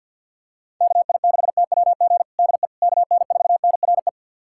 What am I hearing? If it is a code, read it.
Morse code, heard as a keyed tone, about 47 wpm, 700 Hz. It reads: KI6TWG DE KN4NRE
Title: USA Callsign DE Callsign - 6 Characters - 50wpm